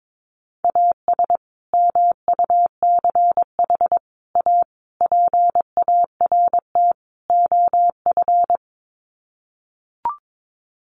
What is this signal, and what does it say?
Morse code, heard as a keyed tone, about 22 wpm, 700 Hz. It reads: AS MUCH A PART OF
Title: Sets of 5 Words Encore2 22wpm